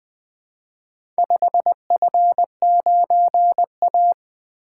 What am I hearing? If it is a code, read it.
Morse code, heard as a keyed tone, about 20 wpm, 700 Hz. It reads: 5F9A
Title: Four Letter-Number Combinations - 20wpm